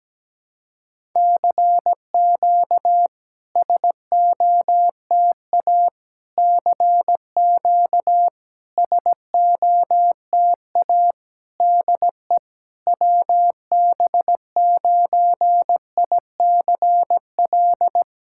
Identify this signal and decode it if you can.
Morse code, heard as a keyed tone, about 17 wpm, 690 Hz. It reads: CQ SOTA CQ SOTA DE WB9ICL
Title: SOTA QSO Practice 17wpm